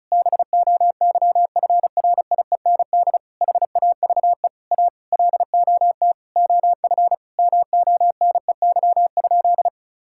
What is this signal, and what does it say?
Morse code, heard as a keyed tone, about 36 wpm, 700 Hz. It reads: BOYFRIEND HAVE A LOT OF MONEY?
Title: Sentences from Top 1000 Words - Review Part 5 - 35wpm